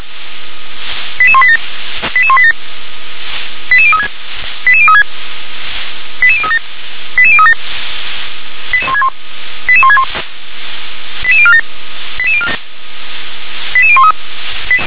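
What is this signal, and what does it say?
Five-tone selective call (ZVEI), heard as quick runs of five tones, tones of about 80 ms. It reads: 88187 88187 88036 88036 88037 88037 88161 88047 88013
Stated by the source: ZVEI five-tone call 88160 received by SDR.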